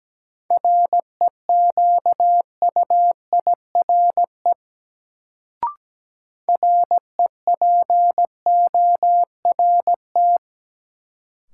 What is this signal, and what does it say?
Morse code, heard as a keyed tone, about 17 wpm, 700 Hz. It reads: REQUIRE REPORT
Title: Sentences from Top 900 Words - New Word Drill - 17wpm